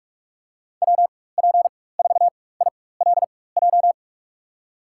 Morse code, heard as a keyed tone, about 46 wpm, spaced as for 21 wpm, 700 Hz. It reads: WP4ILJ
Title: USA Call Signs - 4x Character Spacing - 45wpm